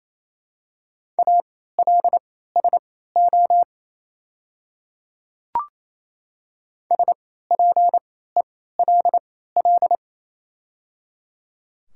Morse code, heard as a keyed tone, about 27 wpm, spaced as for 16 wpm, 700 Hz. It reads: ALSO SPELL
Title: 3x Character Spacing - Top 200 Words - 28wpm